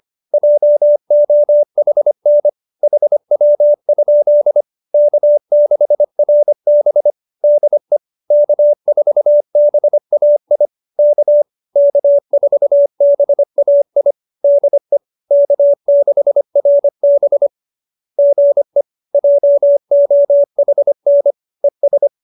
Morse code, heard as a keyed tone, about 25 wpm, 570 Hz. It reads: JOHN HW? K6RB DE K4BAI K K4BAI DE K6RB GE JOHN ES